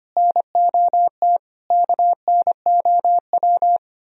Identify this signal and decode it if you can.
Morse code, heard as a keyed tone, about 29 wpm, 700 Hz. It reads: NOT KNOW
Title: Speed-Racing - Sets of 4 Words 25wpm